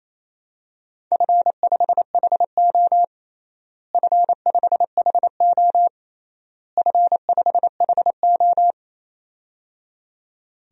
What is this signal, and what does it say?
Morse code, heard as a keyed tone, about 28 wpm, 700 Hz. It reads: F5HO F5HO F5HO
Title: DXCC-100 Call Signs - 4 Characters - Repeated 3x - 28wpm